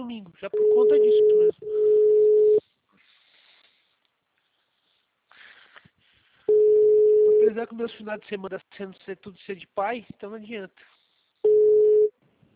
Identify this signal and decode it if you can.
Ringback tone (Europe, 425 Hz), with speech.
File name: out- -2007- - - .173